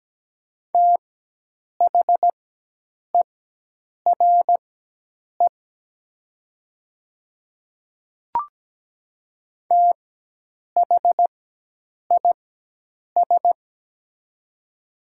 Morse code, heard as a keyed tone, about 17 wpm, spaced as for 8 wpm, 700 Hz. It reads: THERE THIS
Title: 4x Character Spacing - Top 100 Words - 17wpm